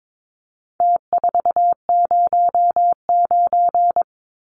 Morse code, heard as a keyed tone, about 22 wpm, 700 Hz. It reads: T409
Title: Numbers - Contest Style - 4-Digit 22wpm